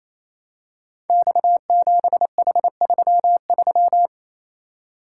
Morse code, heard as a keyed tone, about 28 wpm, 700 Hz. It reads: X7H33
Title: Code Groups - 5 Letter:Numbers - 28wpm